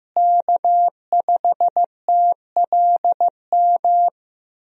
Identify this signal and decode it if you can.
Morse code, heard as a keyed tone, about 15 wpm, 700 Hz. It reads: K5TLM
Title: USA Call Signs - 5 Characters - Repeated 3x - 15wpm